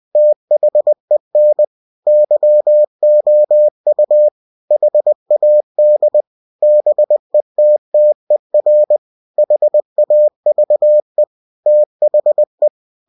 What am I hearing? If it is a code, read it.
Morse code, heard as a keyed tone, about 20 wpm, 600 Hz. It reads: THEN YOU HAD BETTER HAVE THE
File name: cw-books / winnie-the-pooh / winnie-20wpm-page-0006